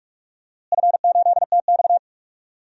Morse code, heard as a keyed tone, about 43 wpm, 700 Hz. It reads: F8TX
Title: DXCC-100 Call Signs - 4 Characters - Repeated 2x - 45wpm